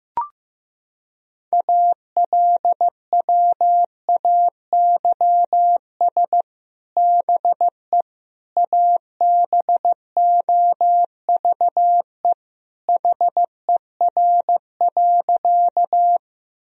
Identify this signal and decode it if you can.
Morse code, heard as a keyed tone, about 15 wpm, 700 Hz. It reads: ALWAYS BE ABOVE HER.
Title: Sentences from Top 1000 Words - Review Part 2 - 15wpm